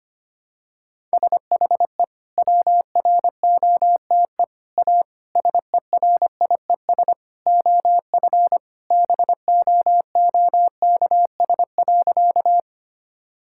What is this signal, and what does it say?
Morse code, heard as a keyed tone, about 25 wpm, 700 Hz. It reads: SHE WROTE A SERIES OF BOOKS.
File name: practice / 25wpm / Sentences with 3-Word Phrases - 25wpm